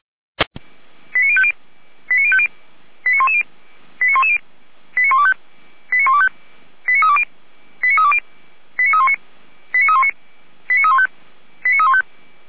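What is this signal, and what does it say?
Five-tone selective call (ZVEI), heard as quick runs of five tones, tones of about 70 ms. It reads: 89050 89050 89110 89110 89125 89125 89320 89320 89319 89319 89315 89315
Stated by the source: ZVEI five-tone call 89187 received on SDR.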